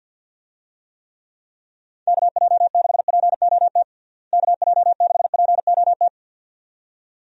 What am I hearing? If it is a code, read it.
Morse code, heard as a keyed tone, about 51 wpm, 700 Hz. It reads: KJ6POT KJ6POT
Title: USA Call Signs - 6 Characters - Repeated 2x - 50wpm